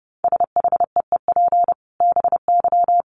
Morse code, heard as a keyed tone, about 30 wpm, 700 Hz. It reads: SHEEP BY
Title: Dit Challenge 30wpm